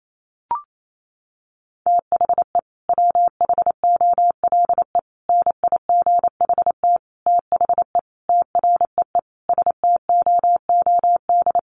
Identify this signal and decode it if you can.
Morse code, heard as a keyed tone, about 28 wpm, 700 Hz. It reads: THE WHOLE NIGHT THE TREE STOOD
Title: Sentences from Top 500 Words 28wpm